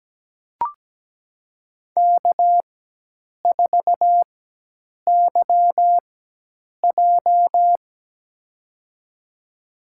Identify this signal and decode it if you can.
Morse code, heard as a keyed tone, about 17 wpm, spaced as for 8 wpm, 700 Hz. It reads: K4YJ
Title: USA Call Signs - 4x Character Spacing - 17wpm